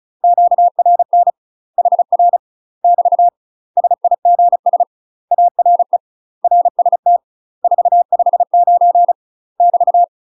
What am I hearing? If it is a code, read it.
Morse code, heard as a keyed tone, about 35 wpm, 700 Hz. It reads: QRN HR <BT> SIGS ARE RST 459 <BT>
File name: UK MASTER QSOs TRAINING / 35WPM / UK MASTER QSOs 0040